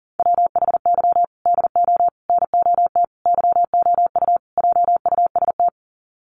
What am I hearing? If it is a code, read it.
Morse code, heard as a keyed tone, about 38 wpm, 700 Hz. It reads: WHY DO NOT YOU JUST
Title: Sets of 5 Words 40wpm